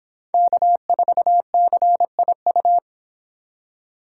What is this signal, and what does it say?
Morse code, heard as a keyed tone, about 25 wpm, 700 Hz. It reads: K4CIU